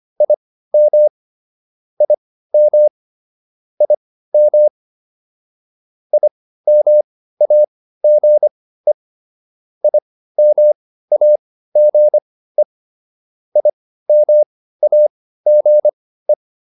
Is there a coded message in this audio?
Yes: Morse code from 0.2 to 16.3 s, heard as a keyed tone, about 25 wpm, spaced as for 15 wpm, 600 Hz. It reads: IM IM IM IMAGE IMAGE IMAGE